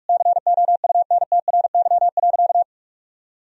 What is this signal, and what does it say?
Morse code, heard as a keyed tone, about 42 wpm, 700 Hz. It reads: COUNTRY.
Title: Sentences from Top 1000 Words - Review Part 3 - 45wpm